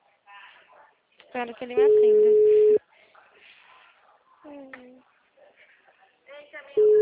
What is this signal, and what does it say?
Ringback tone (Europe, 425 Hz), with speech.